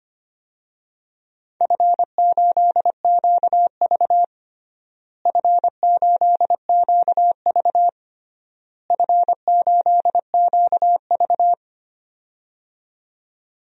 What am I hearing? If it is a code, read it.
Morse code, heard as a keyed tone, about 25 wpm, 700 Hz. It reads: F8QV F8QV F8QV
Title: DXCC-100 Call Signs - 4 Characters - Repeated 3x - 25wpm